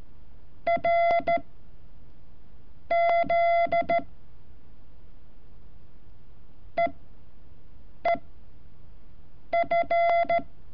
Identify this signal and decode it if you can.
Morse code, heard as a keyed tone, about 13 wpm, spaced as for 5 wpm, 680 Hz. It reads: R Z EEF
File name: curso / ejercicio 31